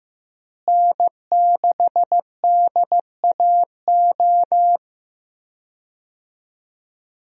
Morse code, heard as a keyed tone, about 15 wpm, 700 Hz. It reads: N6DAO
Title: USA Call Signs - 5 Characters - Repeated 2x - 15wpm